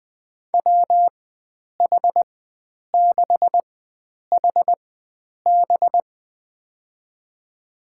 Morse code, heard as a keyed tone, about 20 wpm, spaced as for 9 wpm, 700 Hz. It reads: WH6HB
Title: USA Call Signs - 5 Characters - 4x Character Spacing - 20wpm